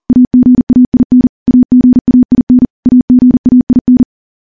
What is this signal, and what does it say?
Morse code, heard as a keyed tone, about 40 wpm, 250 Hz. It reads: AGAIN AGAIN AGAIN
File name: morse / morse-training-40WPM